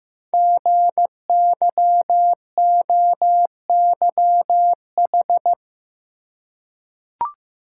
Morse code, heard as a keyed tone, about 15 wpm, 700 Hz. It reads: GYOYH
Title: Code Groups - 5 Letter:Numbers - 15wpm